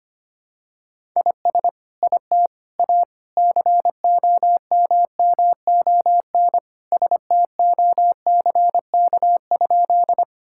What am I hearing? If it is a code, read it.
Morse code, heard as a keyed tone, about 25 wpm, 700 Hz. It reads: IS IT A COMMON STOCK?
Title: Sentences from Top 1000 Words - Review Part 5 - 25wpm